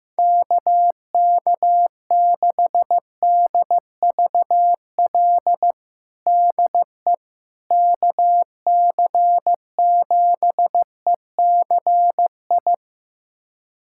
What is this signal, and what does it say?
Morse code, heard as a keyed tone, about 15 wpm, 700 Hz. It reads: KK6DVL DE KC7ECI
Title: USA Callsign DE Callsign - 6 Characters - 15wpm